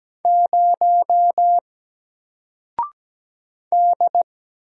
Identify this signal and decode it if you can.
Morse code, heard as a keyed tone, about 17 wpm, 700 Hz. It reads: 0 D
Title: Single Letter-Number 17wpm